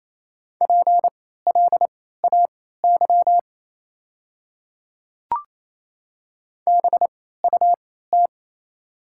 Morse code, heard as a keyed tone, about 28 wpm, spaced as for 16 wpm, 690 Hz. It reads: PLAY BUT
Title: 3x Character Spacing - Top 500 Words - 28wpm